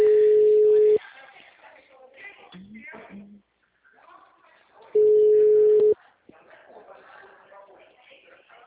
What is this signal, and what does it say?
Ringback tone (Europe, 425 Hz), with speech.
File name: out- -2004- - - .201